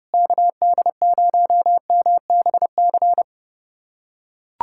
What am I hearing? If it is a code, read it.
Morse code, heard as a keyed tone, about 32 wpm, 700 Hz. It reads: KD0MBC
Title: USA Callsign DE Callsign - 6 Characters - 30wpm